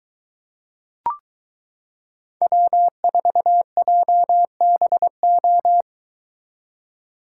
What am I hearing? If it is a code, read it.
Morse code, heard as a keyed tone, about 23 wpm, 700 Hz. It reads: W4JBO